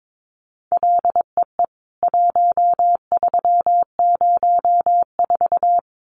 Morse code, heard as a keyed tone, about 22 wpm, 700 Hz. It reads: LEE 1304
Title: CWT Exchange 22wpm